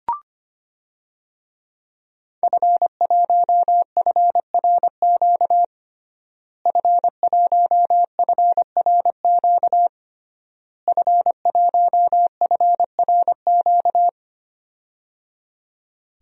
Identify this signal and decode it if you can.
Morse code, heard as a keyed tone, about 25 wpm, 700 Hz. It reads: F1FRQ F1FRQ F1FRQ
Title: DXCC-100 Call Signs - 5 Characters - Repeated 3x - 25wpm